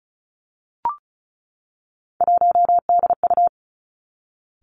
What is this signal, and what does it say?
Morse code, heard as a keyed tone, about 36 wpm, 700 Hz. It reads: 1DU
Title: Three Challenging Letter-Number Combinations 35wpm